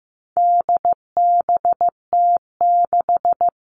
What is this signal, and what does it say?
Morse code, heard as a keyed tone, about 15 wpm, 700 Hz. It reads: DBT6
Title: All Cut Numbers - 4-Digit 15wpm